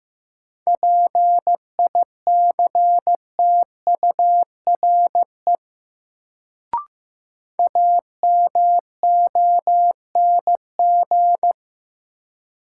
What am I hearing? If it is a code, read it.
Morse code, heard as a keyed tone, about 16 wpm, 700 Hz. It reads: PICTURE AMONG
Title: Sentences from Top 500 Words - New Word Drill - 15wpm